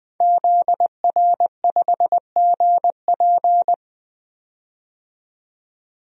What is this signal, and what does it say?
Morse code, heard as a keyed tone, about 20 wpm, 700 Hz. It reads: ZR5GP
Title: DXCC-100 Call Signs - 5 Characters - 20wpm